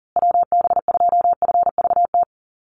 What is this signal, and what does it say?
Morse code, heard as a keyed tone, about 40 wpm, 700 Hz. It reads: WB2FVT
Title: USA Call Signs Repeated 3x 40wpm